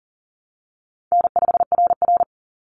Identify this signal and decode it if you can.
Morse code, heard as a keyed tone, about 40 wpm, 700 Hz. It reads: N5RR
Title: Easy Call Signs Repeated 2x - 40wpm